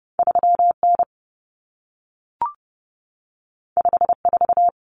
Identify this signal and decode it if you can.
Morse code, heard as a keyed tone, about 30 wpm, 700 Hz. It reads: 3N 54
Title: Common Cut Numbers - 2-Digit 30wpm